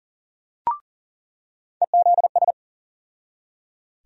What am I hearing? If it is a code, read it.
Morse code, heard as a keyed tone, about 40 wpm, 700 Hz. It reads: EZS
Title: USA Call Sign Postfixes - 3 Letters 40wpm